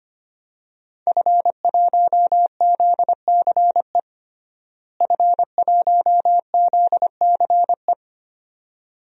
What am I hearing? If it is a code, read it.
Morse code, heard as a keyed tone, about 25 wpm, 700 Hz. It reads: F1ZCE F1ZCE
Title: DXCC-100 Call Signs - 5 Characters - Repeated 2x - 25wpm